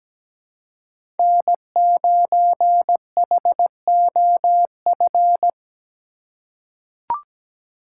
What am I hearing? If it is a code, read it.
Morse code, heard as a keyed tone, about 17 wpm, 700 Hz. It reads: N9HOF
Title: USA Call Signs - 5 Characters - 4x Character Spacing - 17wpm